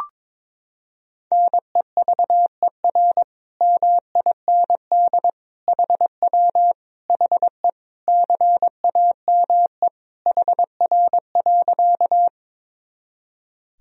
Morse code, heard as a keyed tone, about 22 wpm, 700 Hz. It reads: NEVER MIND HW HE CAME HR.
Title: Sentences from Top 500 Words and Abbreviations 22wpm